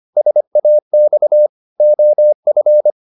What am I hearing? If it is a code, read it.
Morse code, heard as a keyed tone, about 25 wpm, 600 Hz. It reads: SAX OF
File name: the old man and the sea part 1 25wpm